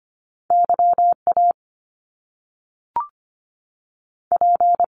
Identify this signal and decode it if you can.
Morse code, heard as a keyed tone, about 25 wpm, 700 Hz. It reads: YA P
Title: Top Two Letter Combinations in Words 25wpm